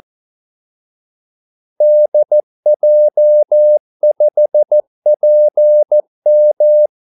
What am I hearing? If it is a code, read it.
Morse code, heard as a keyed tone, about 14 wpm, 600 Hz. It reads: DJ5PM
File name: Callsigns-14wpm